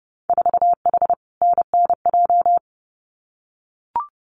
Morse code, heard as a keyed tone, about 30 wpm, 700 Hz. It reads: 4H NNJ
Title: Winter Field Day Exchanges Encore 30wpm